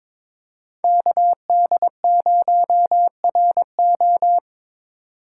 Morse code, heard as a keyed tone, about 22 wpm, 700 Hz. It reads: KD0RO
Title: USA Call Signs - 5 Characters - 3x Character Spacing - 22wpm